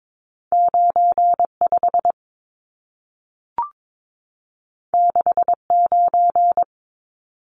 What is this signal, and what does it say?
Morse code, heard as a keyed tone, about 22 wpm, 700 Hz. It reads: 95 69
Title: Numbers - 2-Digit 22wpm